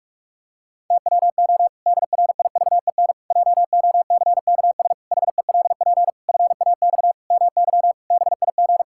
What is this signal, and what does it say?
Morse code, heard as a keyed tone, about 46 wpm, 700 Hz. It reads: TWO DRIVEN JOCKS HELP FAX MY BIG
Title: Pangram Challenge v2 - 45wpm